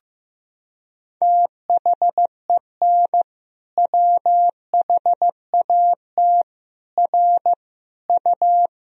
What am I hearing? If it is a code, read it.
Morse code, heard as a keyed tone, about 15 wpm, 700 Hz. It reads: THEN WHAT R U
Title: Sentences from Top 100 Words and Abbreviations 15wpm